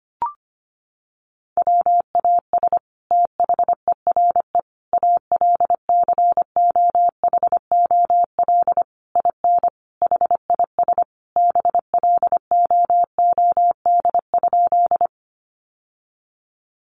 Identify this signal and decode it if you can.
Morse code, heard as a keyed tone, about 25 wpm, 700 Hz. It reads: WAS THERE ALCOHOL IN HIS BLOOD?
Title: Sentences from Top 1400 Words 25wpm